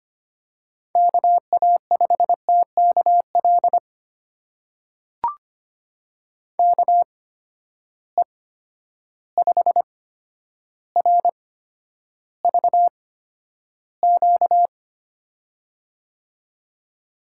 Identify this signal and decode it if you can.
Morse code, heard as a keyed tone, about 25 wpm, spaced as for 11 wpm, 700 Hz. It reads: KA5TKL KE5RVQ
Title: USA Call Signs - 6 Characters - 8x Character Spacing - 25wpm